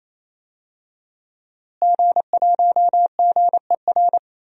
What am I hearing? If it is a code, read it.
Morse code, heard as a keyed tone, about 28 wpm, 700 Hz. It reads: G1GER
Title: DXCC-100 Call Signs - 5 Characters - Repeated 3x - 28wpm